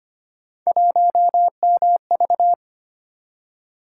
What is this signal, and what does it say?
Morse code, heard as a keyed tone, about 25 wpm, 700 Hz. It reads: 1MV
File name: Three Letter-Number Combinations - 25wpm